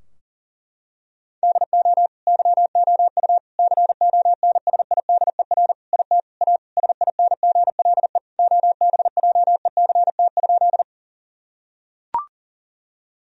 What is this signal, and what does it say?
Morse code, heard as a keyed tone, about 40 wpm, 700 Hz. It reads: DO YOU CONSIDER IT A SINGLE OBJECT?
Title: Sentences from Top 1000 Words - Review Part 4 - 40wpm